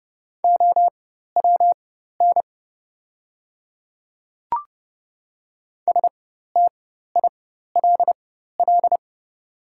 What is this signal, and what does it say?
Morse code, heard as a keyed tone, about 30 wpm, spaced as for 14 wpm, 700 Hz. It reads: OWN STILL
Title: 4x Character Spacing - Top 300 Words - 30wpm